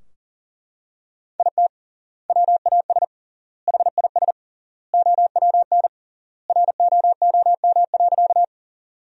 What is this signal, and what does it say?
Morse code, heard as a keyed tone, about 40 wpm, 700 Hz. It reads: IT WAS HIS OWN ROOM.
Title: 3x Word-Spacing - Sentences from Top 300 Words - 40wpm